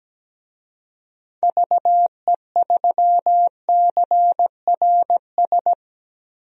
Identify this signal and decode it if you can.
Morse code, heard as a keyed tone, about 17 wpm, 700 Hz. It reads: VE3CRS
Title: DXCC-100 Call Signs - 6 Characters - Repeated 2x - 17wpm